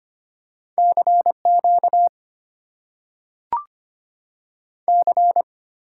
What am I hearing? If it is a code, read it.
Morse code, heard as a keyed tone, about 25 wpm, 700 Hz. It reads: CQ C
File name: Top 90 QSO Elements - 8x Character Spacing - 25wpm